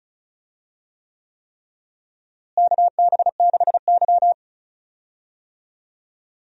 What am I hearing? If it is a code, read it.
Morse code, heard as a keyed tone, about 36 wpm, 700 Hz. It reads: KB6Y
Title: USA Call Signs - 4 Characters - Repeated 3x - 35wpm